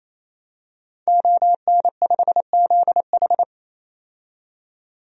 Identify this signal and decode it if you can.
Morse code, heard as a keyed tone, about 28 wpm, 700 Hz. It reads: ON5ZH
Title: DXCC-100 Call Signs - 5 Characters - 28wpm